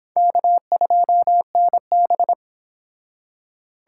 Morse code, heard as a keyed tone, about 26 wpm, 700 Hz. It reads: K2NB
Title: Speed Racing - USA 1x2 Call Signs 22wpm